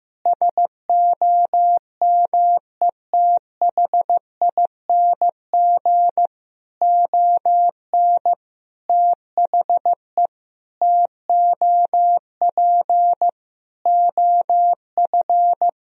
Morse code, heard as a keyed tone, about 15 wpm, 700 Hz. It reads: SOMETHING ON THE TOP OF
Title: Sentences from Top 1000 Words - Review Part 2 - 15wpm